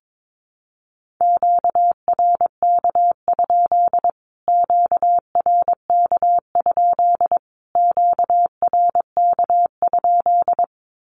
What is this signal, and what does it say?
Morse code, heard as a keyed tone, about 22 wpm, 700 Hz. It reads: QRK? QRK? QRK?
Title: Q-Codes Repeated 3x 22wpm